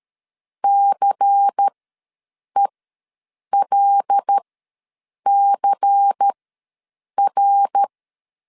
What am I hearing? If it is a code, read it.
Morse code, heard as a keyed tone, about 13 wpm, spaced as for 7 wpm, 800 Hz. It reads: CELCR
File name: LEZIONI / 5 LBVC / 5 LBVC ws 12wpm 3159